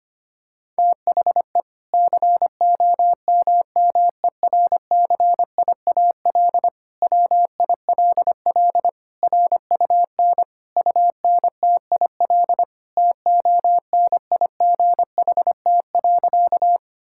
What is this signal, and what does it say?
Morse code, heard as a keyed tone, about 25 wpm, 700 Hz. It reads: THE COMMERCIAL WILL RUN UNTIL TONIGHT.
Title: Sentences from Top 1000 Words - Review Part 5 - 25wpm